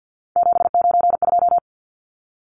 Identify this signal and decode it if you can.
Morse code, heard as a keyed tone, about 53 wpm, 700 Hz. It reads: 792
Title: Numbers - 3-Digit 50wpm